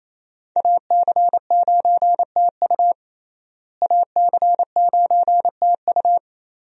Morse code, heard as a keyed tone, about 28 wpm, 700 Hz. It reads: AC9TU AC9TU
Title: USA Call Signs - 5 Characters - Repeated 2x - 28wpm